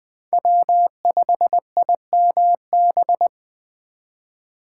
Morse code, heard as a keyed tone, about 20 wpm, 700 Hz. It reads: W5IMB
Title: Speed Racing - USA 1x3 Call Signs 17wpm